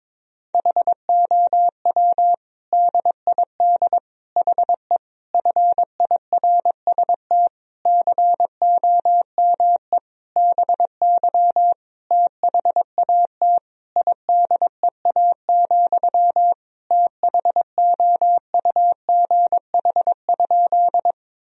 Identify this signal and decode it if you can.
Morse code, heard as a keyed tone, about 22 wpm, 700 Hz. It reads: HOW DID HE FIRST COME BY THAT IDEA, THOUGH?
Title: Sentences from Top 1000 Words - Review Part 2 - 22wpm